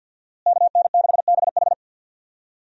Morse code, heard as a keyed tone, about 52 wpm, 700 Hz. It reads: KN6BH
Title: USA Call Signs - 5 Characters - Repeated 2x - 50wpm